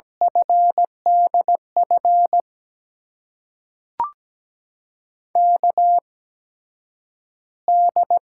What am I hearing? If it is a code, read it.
Morse code, heard as a keyed tone, about 17 wpm, spaced as for 9 wpm, 700 Hz. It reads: FDF KD
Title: USA Call Signs - 6 Characters - 8x Character Spacing - 17wpm